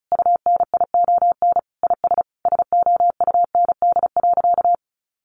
Morse code, heard as a keyed tone, about 34 wpm, 700 Hz. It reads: UNION IS SOUND.
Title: Sentences from Top 1400 Words 35wpm